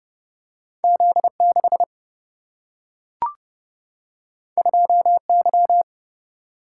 Morse code, heard as a keyed tone, about 30 wpm, 700 Hz. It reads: Z6 2Y
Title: Single-Letter Single-Number 30wpm